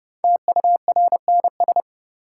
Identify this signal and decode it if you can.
Morse code, heard as a keyed tone, about 30 wpm, 700 Hz. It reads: TURNS
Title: Sets of 5 Words Encore - New Word Drill - 30wpm